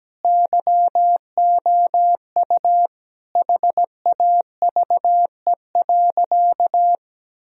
Morse code, heard as a keyed tone, about 17 wpm, 700 Hz. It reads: YOU HAVE.
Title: Sentences from Top 1000 Words - Review Part 3 - 17wpm